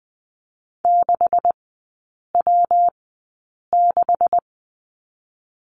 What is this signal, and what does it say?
Morse code, heard as a keyed tone, about 20 wpm, spaced as for 8 wpm, 700 Hz. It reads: 6W6
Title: 127 Lesson 21e - Number 6 - Three Characters 20wpm